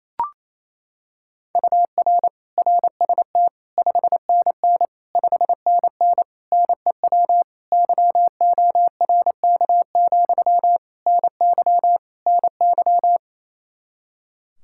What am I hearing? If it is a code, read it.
Morse code, heard as a keyed tone, about 28 wpm, 700 Hz. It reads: UR RST 5NN 5NN NEW YORK, NY NY
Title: Practice QSO - Exchange 1 - Beginning - 28wpm